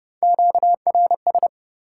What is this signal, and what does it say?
Morse code, heard as a keyed tone, about 29 wpm, 700 Hz. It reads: QRS
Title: Top 90 QSO Elements 30wpm